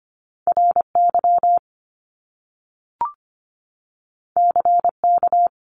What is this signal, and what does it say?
Morse code, heard as a keyed tone, about 25 wpm, 700 Hz. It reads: RY CK
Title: Top Two Letter Combinations in Words 25wpm